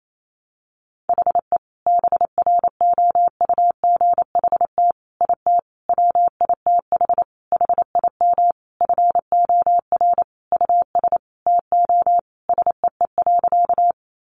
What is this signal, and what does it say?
Morse code, heard as a keyed tone, about 28 wpm, 700 Hz. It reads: HE BROUGHT IT WITH HIM FOR US TO SEE.
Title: Sentences from Top 500 Words 28wpm